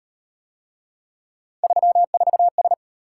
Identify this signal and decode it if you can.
Morse code, heard as a keyed tone, about 38 wpm, 700 Hz. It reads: 34S